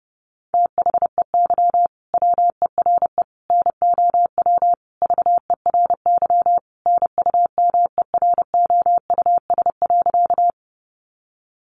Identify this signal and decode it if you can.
Morse code, heard as a keyed tone, about 30 wpm, 700 Hz. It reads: THEY WERE NOW VERY NUMEROUS.
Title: Sentences from Top 1400 Words 30wpm